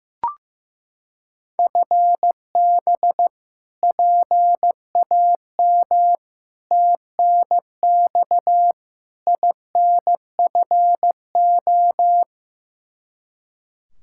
Morse code, heard as a keyed tone, about 16 wpm, 700 Hz. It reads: FB PAM TNX INFO
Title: K1USN SST QSO Practice 15wpm